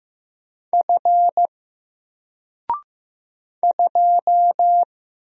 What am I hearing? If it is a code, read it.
Morse code, heard as a keyed tone, about 15 wpm, 700 Hz. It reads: F 2
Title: Single Character (Letter:Number:Punctuation:Prosign) 15wpm